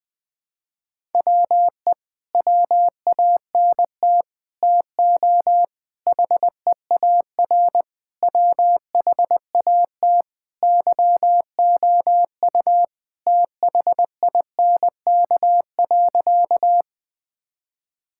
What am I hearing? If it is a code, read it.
Morse code, heard as a keyed tone, about 21 wpm, 700 Hz. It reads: WE WANT TO HEAR WHAT YOU THINK.
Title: Sentences with 6-Word Phrases - 20wpm